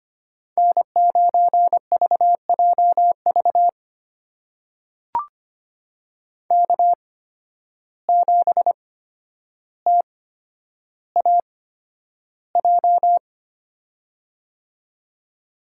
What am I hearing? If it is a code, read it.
Morse code, heard as a keyed tone, about 25 wpm, spaced as for 11 wpm, 700 Hz. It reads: N9VJV K7TAJ
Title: USA Call Signs - 5 Characters - 8x Character Spacing - 25wpm